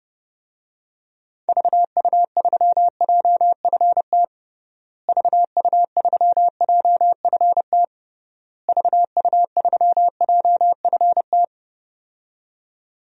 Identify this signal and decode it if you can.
Morse code, heard as a keyed tone, about 30 wpm, 700 Hz. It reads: VU3JFT VU3JFT VU3JFT
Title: DXCC-100 Call Signs - 6 Characters - Repeated 3x - 30wpm